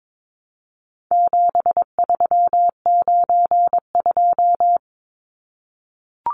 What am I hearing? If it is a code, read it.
Morse code, heard as a keyed tone, about 22 wpm, 700 Hz. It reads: 7392
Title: Numbers - 4-Digit 22wpm